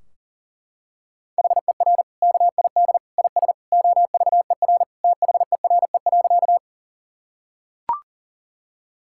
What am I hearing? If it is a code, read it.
Morse code, heard as a keyed tone, about 40 wpm, 700 Hz. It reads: HER KID IS OVER THERE.
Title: Sentences from Top 1600 Words 40wpm